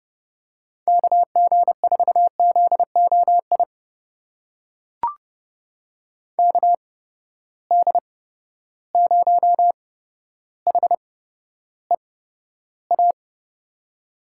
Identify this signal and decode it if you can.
Morse code, heard as a keyed tone, about 29 wpm, spaced as for 13 wpm, 700 Hz. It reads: KG4ZOI KD0HEA
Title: USA Call Signs - 6 Characters - 8x Character Spacing - 30wpm